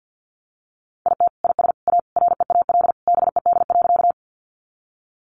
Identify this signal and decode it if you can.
Morse code, heard as a keyed tone, about 50 wpm, 700 Hz. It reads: IT IS A REAL BED.
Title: Sentences from Top 500 Words 50wpm